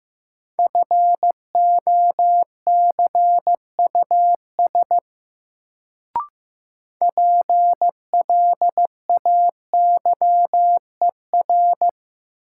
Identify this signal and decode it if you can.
Morse code, heard as a keyed tone, about 15 wpm, 700 Hz. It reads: FOCUS PLAYER
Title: Sentences from Top 1000 Words - Review Part 4 - New Word Drill - 15wpm